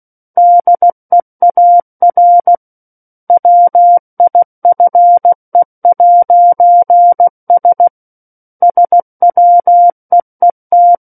Morse code, heard as a keyed tone, about 16 wpm, 700 Hz. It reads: DEAR WIFE'S SWEET
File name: Book One / 07-Chapter 7 - 16 WPM